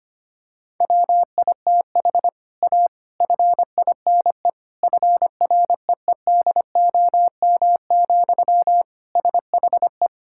Morse code, heard as a keyed tone, about 25 wpm, 700 Hz. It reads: WITH A FINE FREEDOM, SHE
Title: Sentences from Top 1000 Words - Review Part 5 - 25wpm